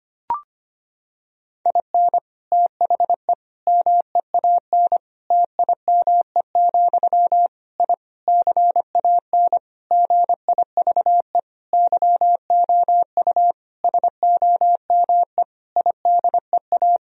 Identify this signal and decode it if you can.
Morse code, heard as a keyed tone, about 25 wpm, 700 Hz. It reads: IN THE MEAN TIME, I CAN GIVE YOU SOME IDEA
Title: Sentences from Top 1000 Words - Review Part 3 - 25wpm